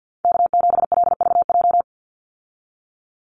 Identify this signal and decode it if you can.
Morse code, heard as a keyed tone, about 50 wpm, 700 Hz. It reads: K7LVJ
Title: Call Signs Repeated 2x - 50wpm